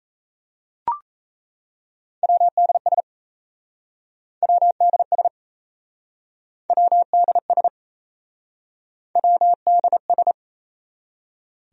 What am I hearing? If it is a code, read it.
Morse code, heard as a keyed tone, about 38 wpm, 700 Hz. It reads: WDS WDS WDS WDS